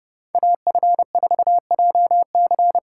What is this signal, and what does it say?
Morse code, heard as a keyed tone, about 30 wpm, 700 Hz. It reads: AF4JC